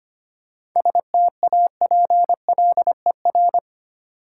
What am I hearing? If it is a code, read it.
Morse code, heard as a keyed tone, about 25 wpm, 700 Hz. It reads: STAPLER
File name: practice / 25wpm / Easy Words with Hints - 25wpm